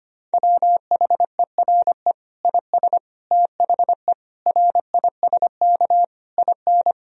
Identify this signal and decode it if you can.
Morse code, heard as a keyed tone, about 25 wpm, 700 Hz. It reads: WHERE IS THE RISK IN
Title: Sentences from Top 1000 Words - Review Part 4 - 25wpm